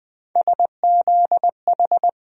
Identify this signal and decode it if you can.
Morse code, heard as a keyed tone, about 20 wpm, 700 Hz. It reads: SZH
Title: USA Call Sign Postfixes - 3 Letters 20wpm